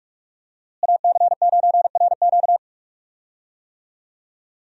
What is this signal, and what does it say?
Morse code, heard as a keyed tone, about 42 wpm, 700 Hz. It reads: AC9RQ
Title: USA Call Signs - 5 Characters - Repeated 2x - 45wpm